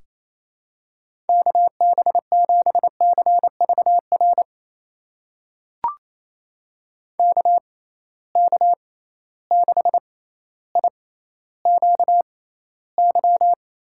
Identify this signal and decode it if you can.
Morse code, heard as a keyed tone, about 28 wpm, spaced as for 14 wpm, 700 Hz. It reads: KB7CVR KK6IQY
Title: USA Call Signs - 6 Characters - 6x Character Spacing - 28wpm